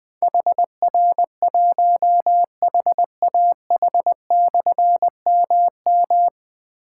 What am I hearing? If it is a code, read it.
Morse code, heard as a keyed tone, about 20 wpm, 700 Hz. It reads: HR1HAH/MM
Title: Difficult Call Signs 20wpm